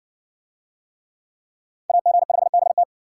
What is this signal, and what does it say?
Morse code, heard as a keyed tone, about 60 wpm, 700 Hz. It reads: AG5BT